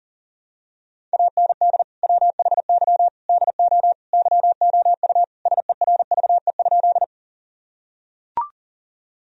Morse code, heard as a keyed tone, about 40 wpm, 700 Hz. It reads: AND WHY DO YOU SERVE?
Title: Sentences from Top 1000 Words - Review Part 2 - 40wpm